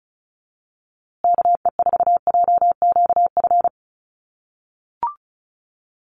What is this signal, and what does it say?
Morse code, heard as a keyed tone, about 34 wpm, 700 Hz. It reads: KE4JQF
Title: USA Call Signs Repeated 3x 35wpm